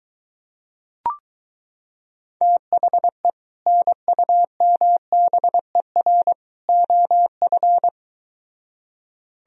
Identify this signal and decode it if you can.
Morse code, heard as a keyed tone, about 23 wpm, 700 Hz. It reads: THE NUMBER OF